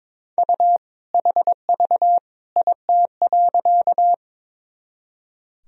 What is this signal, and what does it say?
Morse code, heard as a keyed tone, about 22 wpm, 700 Hz. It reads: U HV IT.
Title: Sentences from Top 100 Words and Abbreviations 22wpm